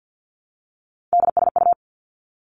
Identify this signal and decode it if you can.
Morse code, heard as a keyed tone, about 50 wpm, 700 Hz. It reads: DSU
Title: Three Challenging Letter-Number Combinations 50wpm